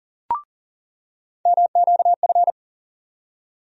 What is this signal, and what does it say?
Morse code, heard as a keyed tone, about 40 wpm, 700 Hz. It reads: MQF
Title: USA Call Sign Postfixes - 3 Letters 40wpm